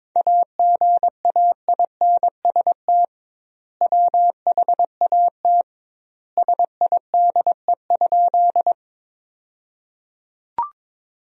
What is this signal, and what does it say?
Morse code, heard as a keyed tone, about 22 wpm, 700 Hz. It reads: AGAINST WHAT SIDE?
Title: 2x Word-Spacing - Sentences from Top 400 Words - 22wpm